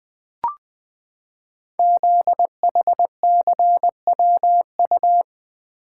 Morse code, heard as a keyed tone, about 20 wpm, 700 Hz. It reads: ZHCWU
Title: Code Groups - 5 Letters - 20wpm